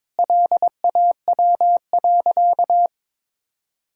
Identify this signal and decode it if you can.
Morse code, heard as a keyed tone, about 22 wpm, 700 Hz. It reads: LAW.
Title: Sentences from Top 1000 Words - Review Part 4 - 22wpm